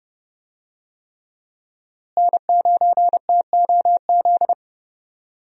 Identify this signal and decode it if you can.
Morse code, heard as a keyed tone, about 30 wpm, 700 Hz. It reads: N9TOZ
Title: USA Call Signs - 5 Characters - Repeated 3x - 30wpm